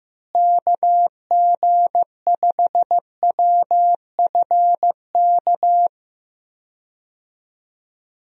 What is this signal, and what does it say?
Morse code, heard as a keyed tone, about 15 wpm, 700 Hz. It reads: KG5WFK
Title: USA Call Signs - 6 Characters - Repeated 2x - 15wpm